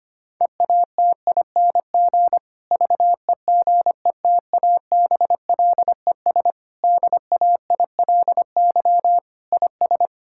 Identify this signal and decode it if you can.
Morse code, heard as a keyed tone, about 25 wpm, 700 Hz. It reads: EATING VEGETABLES DAILY IS
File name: practice / 25wpm / Difficult Sentences with Hints - 25wpm